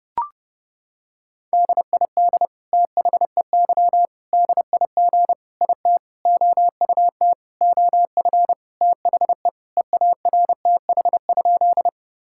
Morse code, heard as a keyed tone, about 30 wpm, 700 Hz. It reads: DID THEY DIG IT OUT OF THE EARTH?
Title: Sentences from Top 1600 Words 30wpm